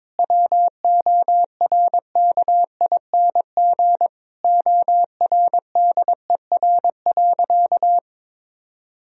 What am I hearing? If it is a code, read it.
Morse code, heard as a keyed tone, about 21 wpm, 700 Hz. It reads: WORKING ORDER.
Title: Sentences from Top 1000 Words - Review Part 4 - 22wpm